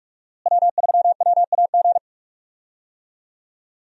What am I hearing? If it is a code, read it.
Morse code, heard as a keyed tone, about 47 wpm, 700 Hz. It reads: W3WAG
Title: USA Call Signs - 5 Characters - Repeated 2x - 45wpm